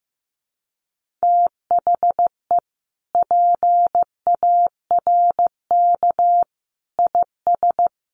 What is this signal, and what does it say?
Morse code, heard as a keyed tone, about 15 wpm, 700 Hz. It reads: THE PARK IS
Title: Sentences from Top 1300 Words 15wpm